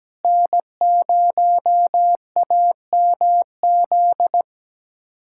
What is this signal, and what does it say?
Morse code, heard as a keyed tone, about 17 wpm, 700 Hz. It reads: N0AMZ
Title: USA Callsign DE Callsign - 5 Characters - 17wpm